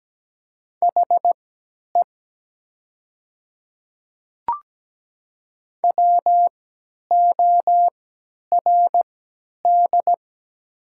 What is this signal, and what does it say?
Morse code, heard as a keyed tone, about 17 wpm, spaced as for 10 wpm, 700 Hz. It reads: HE WORD
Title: 3x Character Spacing - Top 100 Words - 17wpm